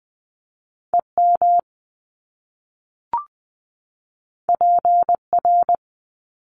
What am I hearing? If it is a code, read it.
Morse code, heard as a keyed tone, about 20 wpm, 700 Hz. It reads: EM PR
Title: Top Two Letter Combinations in Words 20wpm